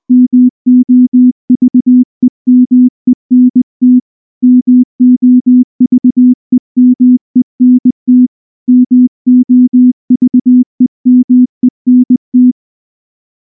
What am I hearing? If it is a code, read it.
Morse code, heard as a keyed tone, about 19 wpm, 250 Hz. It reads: MOVEMENT MOVEMENT MOVEMENT
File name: morse / morse-training-20WPM-4